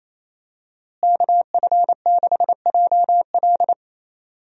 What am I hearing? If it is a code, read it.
Morse code, heard as a keyed tone, about 27 wpm, 700 Hz. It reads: KF6JL
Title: USA Call Signs - 5 Characters - Repeated 3x - 28wpm